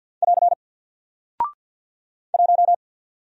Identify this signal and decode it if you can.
Morse code, heard as a keyed tone, about 50 wpm, 700 Hz. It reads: <AR> 1
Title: Single Character (Letter:Number:Punctuation:Prosign) Alt 50wpm